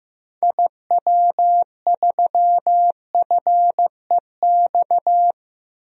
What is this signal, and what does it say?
Morse code, heard as a keyed tone, about 15 wpm, 700 Hz. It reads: IW3FEX
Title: DXCC-100 Call Signs - 6 Characters - 15wpm